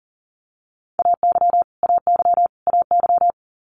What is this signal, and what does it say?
Morse code, heard as a keyed tone, about 40 wpm, 700 Hz. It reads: AY AY AY
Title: Multi-letter Phonemes Repeated 3x 40wpm